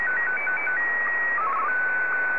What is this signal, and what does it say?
SSTV (slow-scan television), Martin 1.